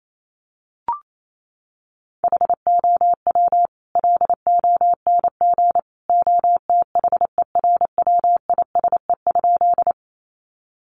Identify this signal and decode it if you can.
Morse code, heard as a keyed tone, about 28 wpm, 700 Hz. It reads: HOW LONG OTHERWISE?
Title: Sentences from Top 1200 Words 28wpm